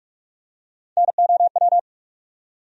Morse code, heard as a keyed tone, about 46 wpm, 700 Hz. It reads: NOW
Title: Top 100 Words Repeated 45wpm